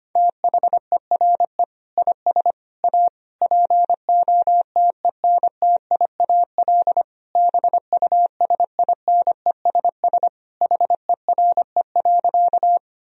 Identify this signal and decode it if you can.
Morse code, heard as a keyed tone, about 25 wpm, 700 Hz. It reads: THERE IS A POTENTIAL BUSINESS HERE.
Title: Sentences from Top 1000 Words - Review Part 4 - 25wpm